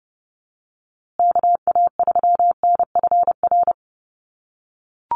Morse code, heard as a keyed tone, about 30 wpm, 700 Hz. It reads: KA3NFR
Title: USA Call Signs Repeated 3x 30wpm